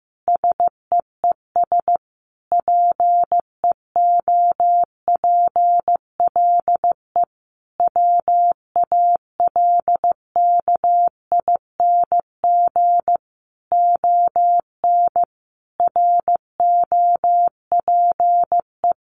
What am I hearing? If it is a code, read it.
Morse code, heard as a keyed tone, about 15 wpm, 700 Hz. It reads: SEES PEOPLE WALKING ON ROPE
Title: Easy Phrases 15wpm